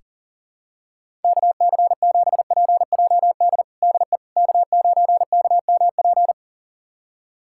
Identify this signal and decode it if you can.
Morse code, heard as a keyed tone, about 40 wpm, 700 Hz. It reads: KC7PJD DE K9KMP
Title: USA Callsign DE Callsign 40wpm